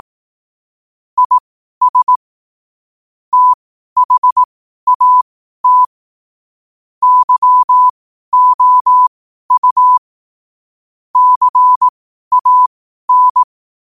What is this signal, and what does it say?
Morse code, heard as a keyed tone, about 18 wpm, spaced as for 12 wpm, 1000 Hz. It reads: IS THAT YOU CAN
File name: qotdfiles / 12wpm